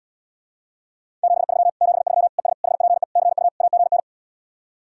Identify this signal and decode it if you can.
Morse code, heard as a keyed tone, about 79 wpm, 700 Hz. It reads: 73 73 ES HPE CU AGN